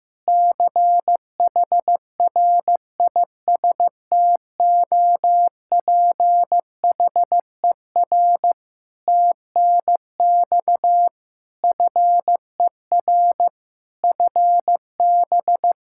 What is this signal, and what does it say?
Morse code, heard as a keyed tone, about 15 wpm, 700 Hz. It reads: CHRISTOPHER TNX FER FB
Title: Practice QSO - Exchange 4 - Ending - 15wpm